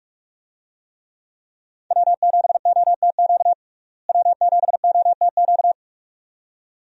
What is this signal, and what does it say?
Morse code, heard as a keyed tone, about 46 wpm, 700 Hz. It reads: W7OTQ W7OTQ
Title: USA Call Signs - 5 Characters - Repeated 2x - 45wpm